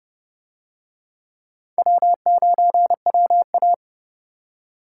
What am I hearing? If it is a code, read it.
Morse code, heard as a keyed tone, about 26 wpm, 700 Hz. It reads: W9WA